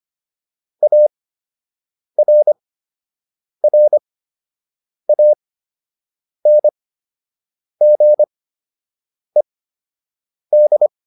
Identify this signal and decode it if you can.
Morse code, heard as a keyed tone, about 25 wpm, spaced as for 7 wpm, 600 Hz. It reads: ARRANGED